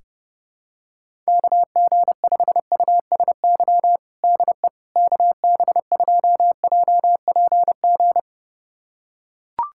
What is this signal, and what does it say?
Morse code, heard as a keyed tone, about 32 wpm, 700 Hz. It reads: KG5USY DE KB2JPG
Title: USA Callsign DE Callsign - 6 Characters - 30wpm